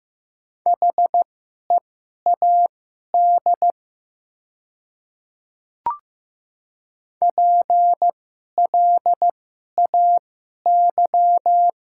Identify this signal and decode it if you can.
Morse code, heard as a keyed tone, about 15 wpm, spaced as for 11 wpm, 700 Hz. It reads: HEAD PLAY
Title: 2x Character Spacing - Top 300 Words - 15wpm